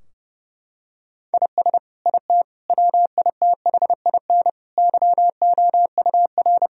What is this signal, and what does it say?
Morse code, heard as a keyed tone, about 32 wpm, 700 Hz. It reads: IS IT WITHIN YOUR
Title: Sentences from Top 1000 Words - Review Part 4 - 30wpm